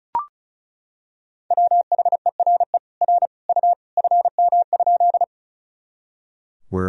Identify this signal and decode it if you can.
Morse code, heard as a keyed tone, about 36 wpm, 700 Hz. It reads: WHERE R U FM?
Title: Sentences from Top 200 Words and Abbreviations 35wpm